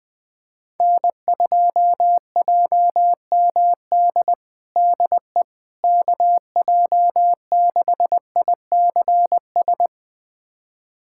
Morse code, heard as a keyed tone, about 20 wpm, 700 Hz. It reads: N2JMD DE KJ6ICS
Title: USA Callsign DE Callsign 20wpm